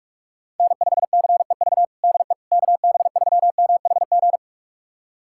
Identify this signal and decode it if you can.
Morse code, heard as a keyed tone, about 42 wpm, 700 Hz. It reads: N5CEV DE KB3MHG
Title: USA Callsign DE Callsign 45wpm